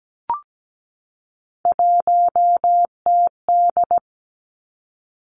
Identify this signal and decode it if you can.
Morse code, heard as a keyed tone, about 17 wpm, 700 Hz. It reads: 1TD
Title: Three Challenging Letter-Number Combinations 17wpm